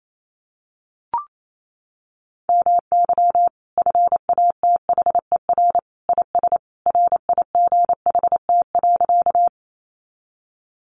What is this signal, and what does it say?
Morse code, heard as a keyed tone, about 28 wpm, 700 Hz. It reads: MY FATHER IS RIGHT.
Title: Sentences from Top 300 Words 28wpm